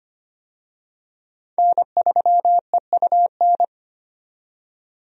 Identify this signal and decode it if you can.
Morse code, heard as a keyed tone, about 25 wpm, 700 Hz. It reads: N3EUN
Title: USA Call Signs - 5 Characters - 25wpm